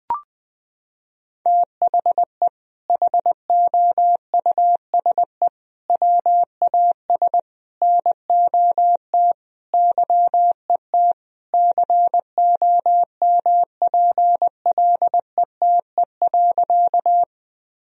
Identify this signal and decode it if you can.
Morse code, heard as a keyed tone, about 20 wpm, 700 Hz. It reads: THE HOUSE WAS NOT YET COMPLETE.
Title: Sentences from Top 1000 Words - Review Part 2 - 20wpm